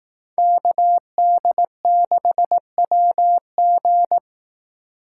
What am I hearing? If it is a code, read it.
Morse code, heard as a keyed tone, about 18 wpm, 700 Hz. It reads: KD6WG
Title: Speed Racing - USA 2x2 Call Signs 15wpm